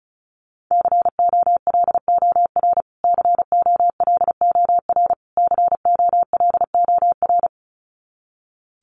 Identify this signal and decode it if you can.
Morse code, heard as a keyed tone, about 34 wpm, 700 Hz. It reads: COLOR COLOR COLOR
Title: Top 300 Words Repeated 35wpm